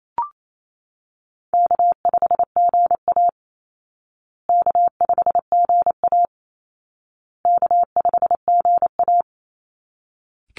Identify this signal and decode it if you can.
Morse code, heard as a keyed tone, about 28 wpm, 700 Hz. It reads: K5GA K5GA K5GA
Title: Easy Call Signs Repeated 3x - 28wpm